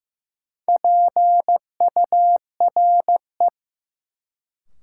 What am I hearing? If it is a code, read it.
Morse code, heard as a keyed tone, about 15 wpm, 700 Hz. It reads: PURE
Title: Sentences from Top 1500 Words - New Word Drill - 15wpm